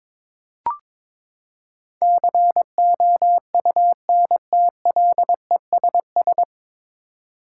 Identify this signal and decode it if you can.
Morse code, heard as a keyed tone, about 22 wpm, 700 Hz. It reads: COUNTLESS
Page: Morse Code Ninja practice 22wpm